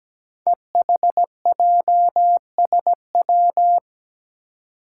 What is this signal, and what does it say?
Morse code, heard as a keyed tone, about 17 wpm, 700 Hz. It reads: EHJSW
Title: Code Groups - 5 Letters - 17wpm